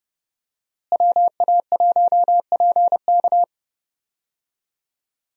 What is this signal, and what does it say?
Morse code, heard as a keyed tone, about 32 wpm, 700 Hz. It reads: WA1PK
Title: USA Call Signs - 5 Characters - Repeated 2x - 30wpm